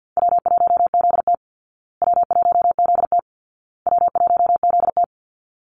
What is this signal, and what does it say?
Morse code, heard as a keyed tone, about 50 wpm, 700 Hz. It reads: W1ZT W1ZT W1ZT
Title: Call Signs Repeated 2x - 50wpm